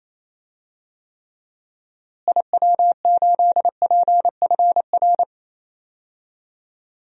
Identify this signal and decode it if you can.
Morse code, heard as a keyed tone, about 28 wpm, 700 Hz. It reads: IW8PFR
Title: DXCC-100 Call Signs - 6 Characters - Repeated 3x - 28wpm